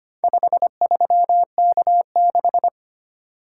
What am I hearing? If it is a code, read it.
Morse code, heard as a keyed tone, about 25 wpm, 700 Hz. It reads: 53K6
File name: practice / 25wpm / Four Letter-Number Combinations - 25wpm